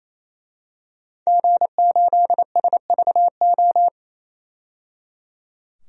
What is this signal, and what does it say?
Morse code, heard as a keyed tone, about 28 wpm, 700 Hz. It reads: G8SVO
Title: DXCC-100 Call Signs - 5 Characters - 28wpm